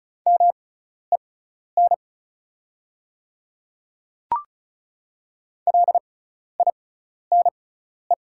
Morse code, heard as a keyed tone, about 34 wpm, spaced as for 12 wpm, 700 Hz. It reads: MEN LINE
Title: 6x Character Spacing - Top 200 Words - 35wpm